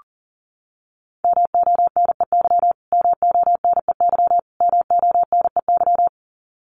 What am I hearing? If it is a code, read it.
Morse code, heard as a keyed tone, about 40 wpm, 700 Hz. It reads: MONEY MONEY MONEY
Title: Top 500 Words Repeated 40wpm